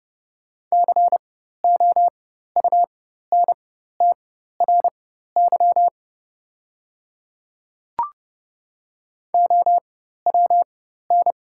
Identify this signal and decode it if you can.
Morse code, heard as a keyed tone, about 30 wpm, spaced as for 14 wpm, 700 Hz. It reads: COUNTRY OWN
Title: 4x Character Spacing - Top 300 Words - 30wpm